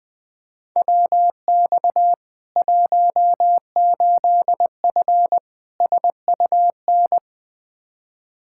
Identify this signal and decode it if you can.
Morse code, heard as a keyed tone, about 20 wpm, 700 Hz. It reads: WX 18F SUN
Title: Practice QSO - Exchange 2 - Info - 20wpm